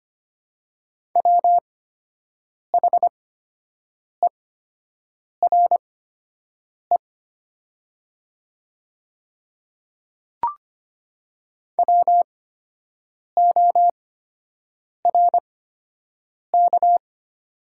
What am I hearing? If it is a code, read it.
Morse code, heard as a keyed tone, about 25 wpm, spaced as for 7 wpm, 690 Hz. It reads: WHERE WORK
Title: 8x Character Spacing - Top 200 Words - 25wpm